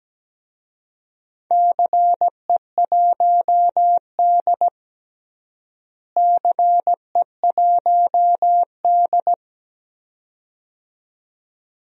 Morse code, heard as a keyed tone, about 17 wpm, 700 Hz. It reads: CE1D CE1D
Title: DXCC-100 Call Signs - 4 Characters - Repeated 2x - 17wpm